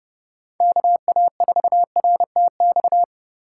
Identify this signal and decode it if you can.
Morse code, heard as a keyed tone, about 28 wpm, 700 Hz. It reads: KA4RTX